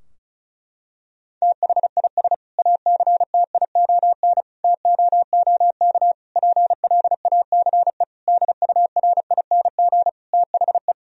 Morse code, heard as a keyed tone, about 36 wpm, 700 Hz. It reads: THIS ACTION TOOK PLACE DURING THE
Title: Sentences from Top 1000 Words - Review Part 4 - 35wpm